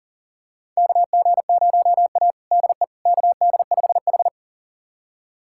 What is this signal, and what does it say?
Morse code, heard as a keyed tone, about 40 wpm, 700 Hz. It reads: KG0A DE KD5H
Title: USA Callsign DE Callsign - 4 Characters - 40wpm